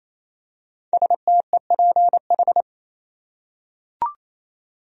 Morse code, heard as a keyed tone, about 28 wpm, 700 Hz. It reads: STEPH
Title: USA Common Names - 28wpm